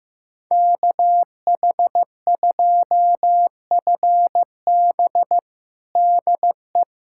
Morse code, heard as a keyed tone, about 15 wpm, 700 Hz. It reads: KH2FB DE
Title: USA Callsign DE Callsign - 5 Characters - 15wpm